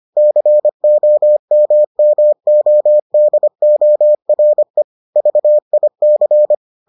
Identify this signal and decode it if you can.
Morse code, heard as a keyed tone, about 25 wpm, 600 Hz. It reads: COMMODORE VIC